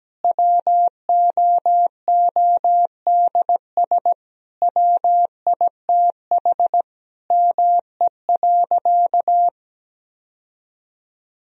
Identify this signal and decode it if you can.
Morse code, heard as a keyed tone, about 17 wpm, 700 Hz. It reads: WOODS WITH ME.
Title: Sentences from Top 1000 Words - Review Part 2 - 17wpm